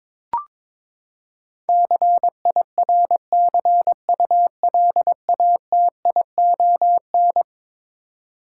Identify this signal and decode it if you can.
Morse code, heard as a keyed tone, about 21 wpm, 700 Hz. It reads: CIRCULATION
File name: practice / 22wpm / Eleven Letter Words - 22wpm